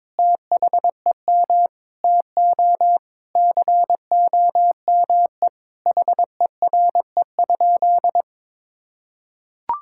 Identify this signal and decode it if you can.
Morse code, heard as a keyed tone, about 22 wpm, 700 Hz. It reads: THEM TO COME HERE?
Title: Sentences from Top 1000 Words - Review Part 5 - 22wpm